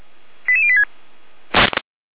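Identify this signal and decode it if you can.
Five-tone selective call (ZVEI), heard as quick runs of five tones, tones of about 70 ms. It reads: 88086
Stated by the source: ZVEI five-tone selective call received by SDR.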